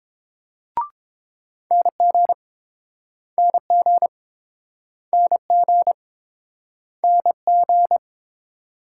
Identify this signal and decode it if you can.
Morse code, heard as a keyed tone, about 30 wpm, 690 Hz. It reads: NG NG NG NG